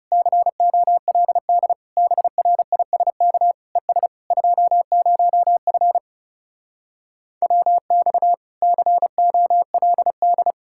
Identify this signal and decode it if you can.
Morse code, heard as a keyed tone, about 40 wpm, 700 Hz. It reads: COLD BRISK ES 20F WX COLD